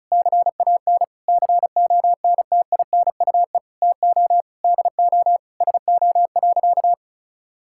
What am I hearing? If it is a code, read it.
Morse code, heard as a keyed tone, about 34 wpm, 700 Hz. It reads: CAN CONTINUE TO DO SO.
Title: Sentences from Top 1000 Words - Review Part 6 - 35wpm